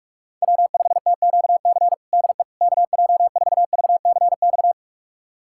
Morse code, heard as a keyed tone, about 46 wpm, 700 Hz. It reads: W5TQC DE KJ4VCX
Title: USA Callsign DE Callsign 45wpm